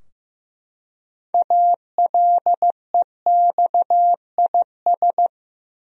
Morse code, heard as a keyed tone, about 15 wpm, 700 Hz. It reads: ALEXIS
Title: USA Common Names - 15wpm